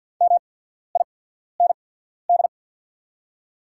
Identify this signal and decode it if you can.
Morse code, heard as a keyed tone, about 48 wpm, spaced as for 14 wpm, 690 Hz. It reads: MIND
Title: 8x Character Spacing - Top 500 Words - 50wpm